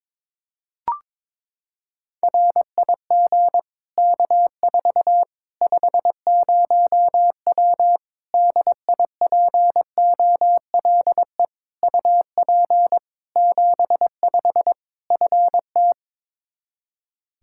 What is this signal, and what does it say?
Morse code, heard as a keyed tone, about 22 wpm, 700 Hz. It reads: RIG K4 50W DIPOLE UP 75 FT
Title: Practice QSO - Exchange 2 - Info - 22wpm